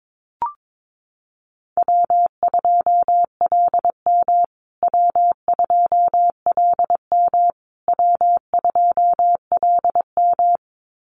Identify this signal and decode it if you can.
Morse code, heard as a keyed tone, about 22 wpm, 700 Hz. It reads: W2LM W2LM W2LM
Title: USA Call Signs Repeated 3x 22wpm